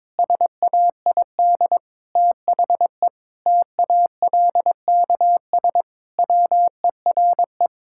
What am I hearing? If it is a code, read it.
Morse code, heard as a keyed tone, about 22 wpm, 700 Hz. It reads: SAID THE TALKS WERE
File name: practice / 22wpm / Sentences with 8-Word Phrases - 22wpm